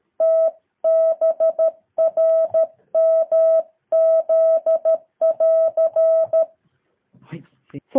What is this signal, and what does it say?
Morse code, heard as a keyed tone, about 13 wpm, 650 Hz. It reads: TBRMZ<AR>
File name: jh8cbh / 2023cw